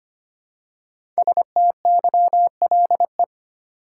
Morse code, heard as a keyed tone, about 25 wpm, 700 Hz. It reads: STYLE
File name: 5 Letter Words - First 3 Letters Given - 25wpm